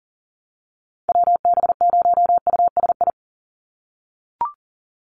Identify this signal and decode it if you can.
Morse code, heard as a keyed tone, about 37 wpm, 700 Hz. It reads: WB0USI
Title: NA SOTA Chaser Callsigns 40wpm